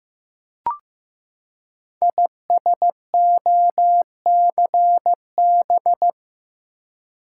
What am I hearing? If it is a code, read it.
Morse code, heard as a keyed tone, about 15 wpm, 700 Hz. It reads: ISOCB
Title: Code Groups - 5 Letters - 15wpm